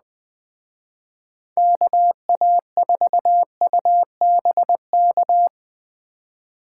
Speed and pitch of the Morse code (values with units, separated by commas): 21 wpm, 700 Hz